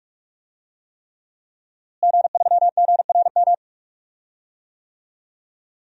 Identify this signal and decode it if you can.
Morse code, heard as a keyed tone, about 46 wpm, 700 Hz. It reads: G3GRM
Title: DXCC-100 Call Signs - 5 Characters - Repeated 3x - 45wpm